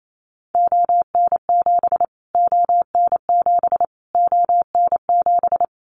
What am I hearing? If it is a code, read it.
Morse code, heard as a keyed tone, about 28 wpm, 700 Hz. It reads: ON7 ON7 ON7
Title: Callsign Prefixes for Top 25 DXCC - Repeated 3x - 28wpm